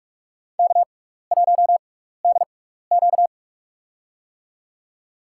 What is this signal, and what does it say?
Morse code, heard as a keyed tone, about 46 wpm, spaced as for 15 wpm, 700 Hz. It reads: K1DQ
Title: USA Call Signs - 4 Characters - 6x Character Spacing - 45wpm